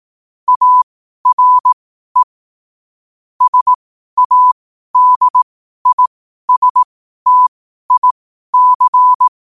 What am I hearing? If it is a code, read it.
Morse code, heard as a keyed tone, about 18 wpm, spaced as for 12 wpm, 1000 Hz. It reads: ARE SADISTIC
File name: qotdfiles / 12wpm